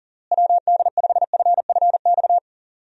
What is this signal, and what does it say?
Morse code, heard as a keyed tone, about 40 wpm, 700 Hz. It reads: WD5FFX
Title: USA Call Signs - 6 Characters - Repeated 3x - 40wpm